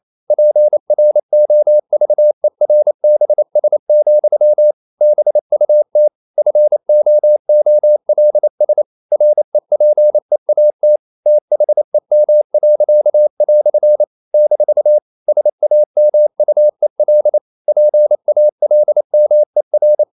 Morse code, heard as a keyed tone, about 28 wpm, 600 Hz. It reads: PROVERBS, BUT FOOLS REPEAT THEM." - SAMUEL PALMER
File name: CW / Practice / EnglishQuotations0228